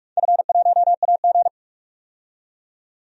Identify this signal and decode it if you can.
Morse code, heard as a keyed tone, about 46 wpm, 700 Hz. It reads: F1AG
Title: DXCC-100 Call Signs - 4 Characters - 45wpm